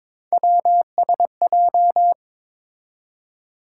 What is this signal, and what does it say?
Morse code, heard as a keyed tone, about 22 wpm, 700 Hz. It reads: WSJ
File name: practice / 22wpm / Three Letter-Number Combinations - 22wpm